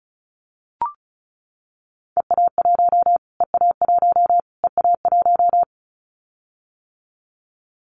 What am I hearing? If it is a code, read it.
Morse code, heard as a keyed tone, about 34 wpm, 700 Hz. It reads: EA1 EA1 EA1
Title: Callsign Prefixes for Top 25 DXCC - Repeated 3x - 35wpm